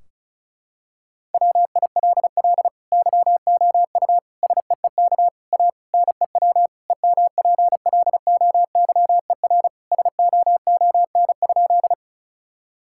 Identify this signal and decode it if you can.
Morse code, heard as a keyed tone, about 36 wpm, 700 Hz. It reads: WILL YOU SEEK A NEW EMPLOYER SOON?
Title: Sentences from Top 1600 Words 35wpm